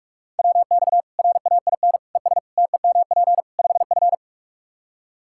Morse code, heard as a keyed tone, about 46 wpm, 700 Hz. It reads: WX RAIN ES TEMP 5F
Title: K1USN SST QSO Practice 45wpm